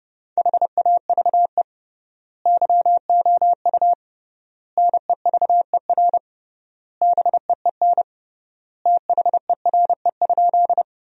Morse code, heard as a keyed tone, about 30 wpm, 700 Hz. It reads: HAVE YOU NEVER BEEN THERE?
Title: 3x Word-Spacing - Sentences from Top 300 Words - 30wpm